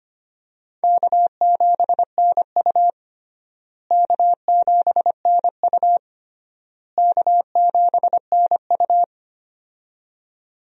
Morse code, heard as a keyed tone, about 25 wpm, 700 Hz. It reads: K7NU K7NU K7NU
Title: USA Call Signs - 4 Characters - Repeated 3x - 25wpm